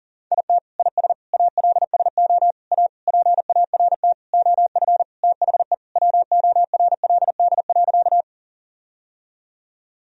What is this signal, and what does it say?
Morse code, heard as a keyed tone, about 40 wpm, 700 Hz. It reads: IT IS ALSO A PART OF THE WORLD.
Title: Sentences from Top 1000 Words - Review Part 1 - 40wpm